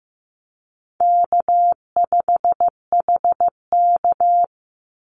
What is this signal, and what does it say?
Morse code, heard as a keyed tone, about 15 wpm, 700 Hz. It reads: K5HK
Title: Easy Call Signs Repeated 2x - 15wpm